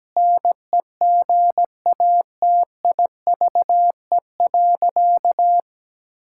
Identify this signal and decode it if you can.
Morse code, heard as a keyed tone, about 17 wpm, 700 Hz. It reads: NEGATIVE.
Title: Sentences from Top 1000 Words - Review Part 6 - 17wpm